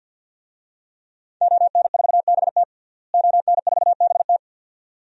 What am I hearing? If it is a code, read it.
Morse code, heard as a keyed tone, about 51 wpm, 700 Hz. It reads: ON4BT ON4BT
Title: DXCC-100 Call Signs - 5 Characters - Repeated 2x - 50wpm